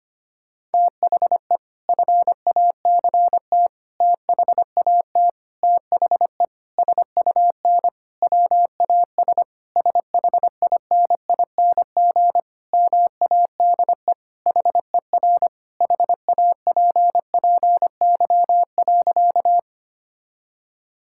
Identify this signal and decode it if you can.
Morse code, heard as a keyed tone, about 25 wpm, 700 Hz. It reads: THE FACT THAT THE SUN WAS SHINING MADE HER HAPPY.
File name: practice / 25wpm / Sentences with 4-Word Phrases - 25wpm